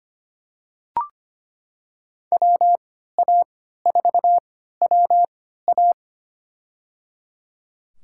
Morse code, heard as a keyed tone, about 25 wpm, spaced as for 14 wpm, 700 Hz. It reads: WA4WA
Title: USA Call Signs - 5 Characters - 3x Character Spacing - 25wpm